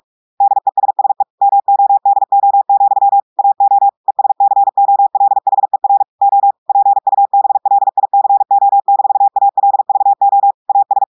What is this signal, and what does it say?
Morse code, heard as a keyed tone, about 46 wpm, 800 Hz. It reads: DESSE MODO, AO ESCOLHER O PUBLICO-ALVO AS
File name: audios / portugais / portugais texte en clair 2 vitesse 45 mots minutes